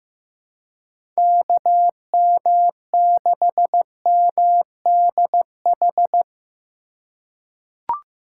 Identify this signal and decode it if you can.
Morse code, heard as a keyed tone, about 15 wpm, 700 Hz. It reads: KM6MDH
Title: USA Call Signs - 2x Character Spacing - 15wpm